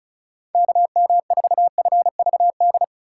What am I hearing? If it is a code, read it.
Morse code, heard as a keyed tone, about 36 wpm, 700 Hz. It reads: KM4FVD
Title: USA Call Signs - 6 Characters - 2x Character Spacing - 35wpm